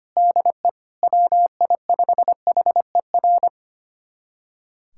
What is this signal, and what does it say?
Morse code, heard as a keyed tone, about 25 wpm, 700 Hz. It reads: DE WI5HER
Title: USA Callsign DE Callsign 25wpm